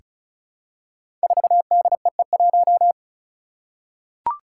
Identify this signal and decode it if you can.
Morse code, heard as a keyed tone, about 36 wpm, 700 Hz. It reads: 4DEE1
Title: Five Challenging Letter-Number Combinations 35wpm